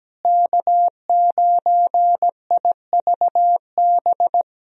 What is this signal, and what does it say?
Morse code, heard as a keyed tone, about 17 wpm, 700 Hz. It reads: K9IVB
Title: USA Call Signs - 5 Characters - Repeated 3x - 17wpm